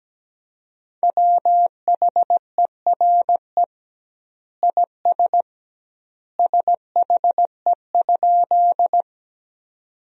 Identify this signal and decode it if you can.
Morse code, heard as a keyed tone, about 17 wpm, 700 Hz. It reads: WHERE IS SHE?
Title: 2x Word-Spacing - Sentences from Top 200 Words - 17wpm